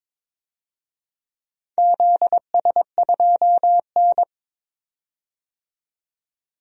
Morse code, heard as a keyed tone, about 22 wpm, 700 Hz. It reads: ZS2N
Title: DXCC-100 Call Signs - 4 Characters - 22wpm